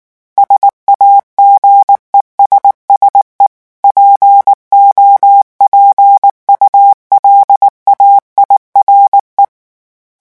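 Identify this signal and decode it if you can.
Morse code, heard as a keyed tone, about 19 wpm, 800 Hz. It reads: SAGESSE POPULAIRE
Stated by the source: French Morse revision 8 drill of random five-letter groups at 18 words per minute.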